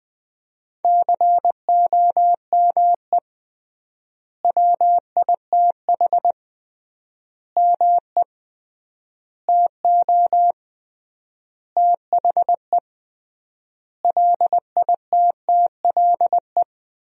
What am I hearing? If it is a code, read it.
Morse code, heard as a keyed tone, about 20 wpm, 700 Hz. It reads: COME WITH ME TO THE LITTLE
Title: 3x Word-Spacing - Sentences from Top 200 Words - 20wpm